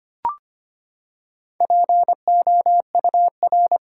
Morse code, heard as a keyed tone, about 25 wpm, 700 Hz. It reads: POUR
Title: Four Letter Words - 25wpm